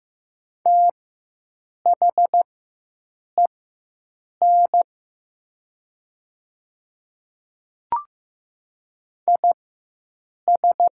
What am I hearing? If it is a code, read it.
Morse code, heard as a keyed tone, about 15 wpm, spaced as for 7 wpm, 700 Hz. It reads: THEN IS
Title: 4x Character Spacing - Top 100 Words - 15wpm